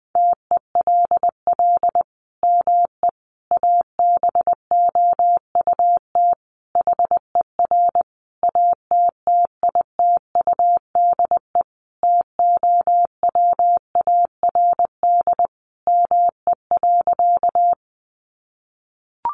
Morse code, heard as a keyed tone, about 20 wpm, 700 Hz. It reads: TELL ME ABOUT HER ATTITUDE TOWARD ME.
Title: Sentences from Top 1300 Words 20wpm